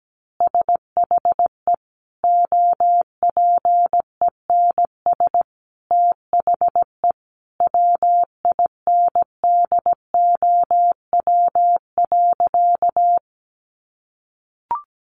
Morse code, heard as a keyed tone, about 17 wpm, 700 Hz. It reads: SHE OPENS THE WINDOW.
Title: Additional Easy Phrases 17wpm